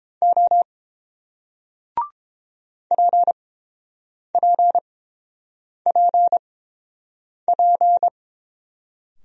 Morse code, heard as a keyed tone, about 28 wpm, 690 Hz. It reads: O P P P P